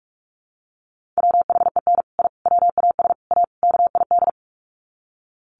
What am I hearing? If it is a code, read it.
Morse code, heard as a keyed tone, about 44 wpm, 700 Hz. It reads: WHEN I WAS A KID
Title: Sets of 5 Words 45wpm